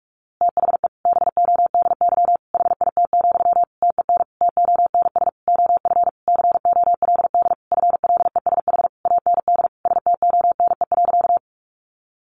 Morse code, heard as a keyed tone, about 44 wpm, 700 Hz. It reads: THE BODY HIT, TEN TONS OF COLD FLESH AND STONE.
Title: A Sound Of Thunder 45wpm